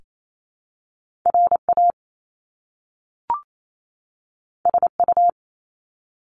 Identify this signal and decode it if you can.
Morse code, heard as a keyed tone, about 28 wpm, 700 Hz. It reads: RA SU
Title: Top Two Letter Combinations in Words 28wpm